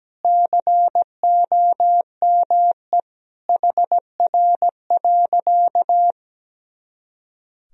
Morse code, heard as a keyed tone, about 17 wpm, 700 Hz. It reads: COME HR.
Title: Sentences from Top 300 Words and Abbreviations 17wpm